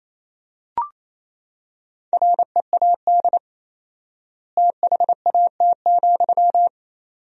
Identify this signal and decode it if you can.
Morse code, heard as a keyed tone, about 28 wpm, 700 Hz. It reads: READ THAT,
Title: 4x Word-Spacing - Sentences from Top 200 Words - 28wpm